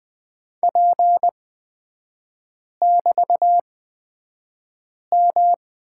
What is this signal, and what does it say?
Morse code, heard as a keyed tone, about 20 wpm, spaced as for 5 wpm, 700 Hz. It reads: P<BT>M
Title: 224 Lesson 36e - BT - Three Characters 20wpm